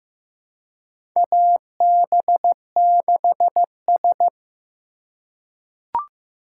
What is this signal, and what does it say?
Morse code, heard as a keyed tone, about 15 wpm, 700 Hz. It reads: AB6S
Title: USA Call Signs - 4 Characters - 4x Character Spacing - 15wpm